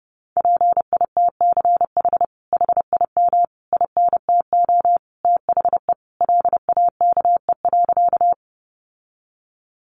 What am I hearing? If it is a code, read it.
Morse code, heard as a keyed tone, about 30 wpm, 700 Hz. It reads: PITCH HIM INTO THE LAKE.
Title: Sentences from Top 1300 Words 30wpm